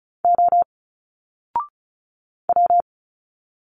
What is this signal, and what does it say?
Morse code, heard as a keyed tone, about 36 wpm, 700 Hz. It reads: O W
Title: Single Letters 35wpm